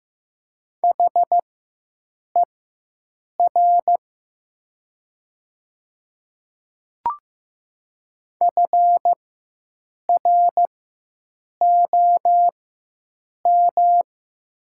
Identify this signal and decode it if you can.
Morse code, heard as a keyed tone, about 15 wpm, spaced as for 7 wpm, 700 Hz. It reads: HER FROM
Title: 4x Character Spacing - Top 100 Words - 15wpm